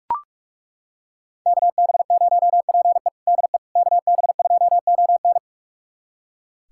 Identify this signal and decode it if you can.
Morse code, heard as a keyed tone, about 42 wpm, 700 Hz. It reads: KB0PE DE KB2ON
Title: USA Callsign DE Callsign - 5 Characters - 45wpm